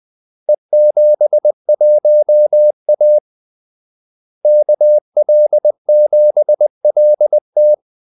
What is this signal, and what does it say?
Morse code, heard as a keyed tone, about 20 wpm, 600 Hz. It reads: E71A KL7LT
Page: Kaldesignaler 20 wpm